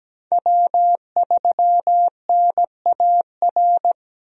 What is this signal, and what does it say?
Morse code, heard as a keyed tone, about 17 wpm, 700 Hz. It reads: W3NAR
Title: USA Call Signs - 5 Characters - Repeated 3x - 17wpm